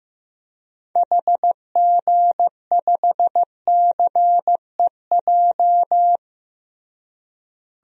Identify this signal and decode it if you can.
Morse code, heard as a keyed tone, about 15 wpm, 700 Hz. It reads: HG5CEJ
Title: DXCC-100 Call Signs - 6 Characters - Repeated 3x - 15wpm